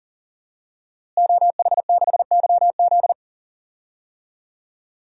Morse code, heard as a keyed tone, about 40 wpm, 700 Hz. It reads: OH6YZ
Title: DXCC-100 Call Signs - 5 Characters - Repeated 2x - 40wpm